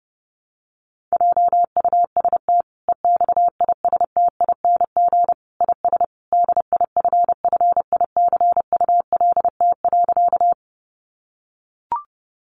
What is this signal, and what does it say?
Morse code, heard as a keyed tone, about 30 wpm, 700 Hz. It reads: JUST EXISTING IS DIFFICULT.
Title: Sentences from Top 1500 Words 30wpm